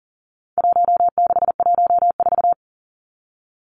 Morse code, heard as a keyed tone, about 40 wpm, 700 Hz. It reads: 1614
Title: Numbers - Contest Style - 4-Digit 40wpm